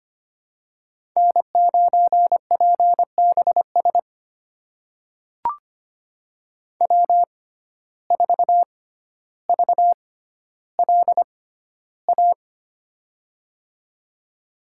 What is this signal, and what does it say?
Morse code, heard as a keyed tone, about 25 wpm, spaced as for 13 wpm, 700 Hz. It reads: N9PBS W4VLA
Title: USA Call Signs - 5 Characters - 6x Character Spacing - 25wpm